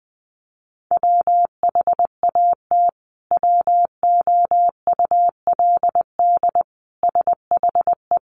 Morse code, heard as a keyed tone, about 20 wpm, 700 Hz. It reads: WHAT WOULD SHE
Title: Sentences from Top 100 Words 20wpm